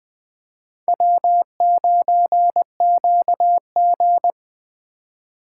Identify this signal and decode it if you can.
Morse code, heard as a keyed tone, about 25 wpm, 700 Hz. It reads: W9QG
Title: Speed Racing - USA Call Signs 20wpm